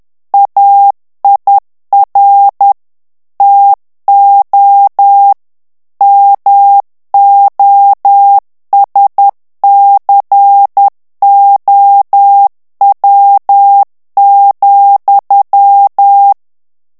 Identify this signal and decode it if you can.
Morse code, heard as a keyed tone, about 10 wpm, 800 Hz. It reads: AIR TO MOSCOW,
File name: morse mp3 / ANGLAIS texte clair 2 vitesse 10 mots minute